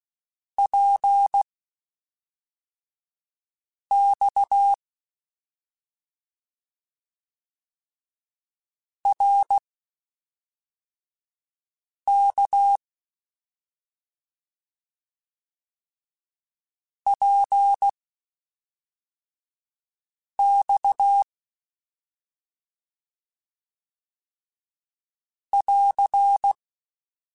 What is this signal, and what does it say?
Morse code, heard as a keyed tone, about 16 wpm, spaced as for 3 wpm, 800 Hz. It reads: PX RK PX <AR>